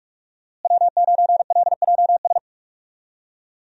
Morse code, heard as a keyed tone, about 47 wpm, 700 Hz. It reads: W9LJS
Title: USA Call Signs - 5 Characters - Repeated 3x - 45wpm